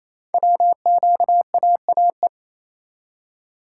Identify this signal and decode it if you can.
Morse code, heard as a keyed tone, about 28 wpm, 700 Hz. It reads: WQAAE
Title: Code Groups - 5 Letter:Numbers - 28wpm